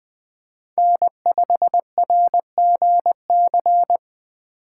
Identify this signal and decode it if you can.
Morse code, heard as a keyed tone, about 20 wpm, 700 Hz. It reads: N5RGC
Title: USA Call Signs - 5 Characters - 3x Character Spacing - 20wpm